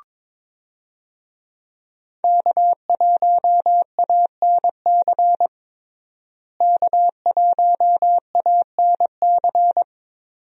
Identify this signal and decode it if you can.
Morse code, heard as a keyed tone, about 22 wpm, 700 Hz. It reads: K1ANC K1ANC
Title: USA Call Signs - 5 Characters - Repeated 2x - 22wpm